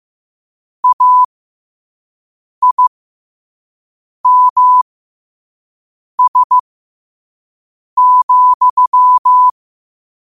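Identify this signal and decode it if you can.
Morse code, heard as a keyed tone, about 15 wpm, spaced as for 5 wpm, 1000 Hz. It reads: AIMS,
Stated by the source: Morse code sent at 5 wpm.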